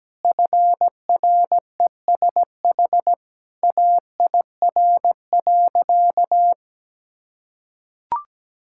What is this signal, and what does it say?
Morse code, heard as a keyed tone, about 17 wpm, 700 Hz. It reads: FRESH AIR.
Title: Sentences from Top 1600 Words 17wpm